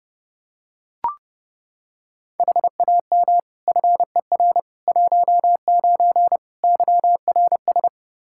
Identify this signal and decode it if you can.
Morse code, heard as a keyed tone, about 30 wpm, 700 Hz. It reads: HAM FER 19 YRS
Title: Practice QSO - Exchange 3 - More Info - 30wpm